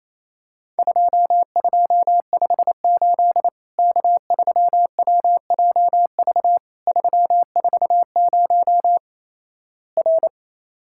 Morse code, heard as a keyed tone, about 28 wpm, 700 Hz. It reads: 2258 K3WJV 340 R
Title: Worked All Europe DX Contest - QTC Exchange - 28wpm